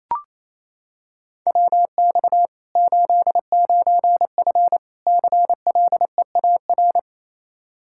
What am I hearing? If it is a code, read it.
Morse code, heard as a keyed tone, about 28 wpm, 700 Hz. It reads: WX 89F CLEAR
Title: Practice QSO - Exchange 2 - Info - 28wpm